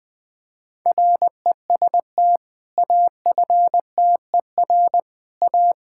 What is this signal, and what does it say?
Morse code, heard as a keyed tone, about 21 wpm, 700 Hz. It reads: REST AFTER A
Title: Sentences with 2-Word Phrases - 20wpm